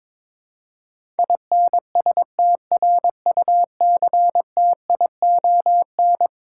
Morse code, heard as a keyed tone, about 22 wpm, 700 Hz. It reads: INSTRUCTION
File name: practice / 22wpm / Eleven Letter Words - 22wpm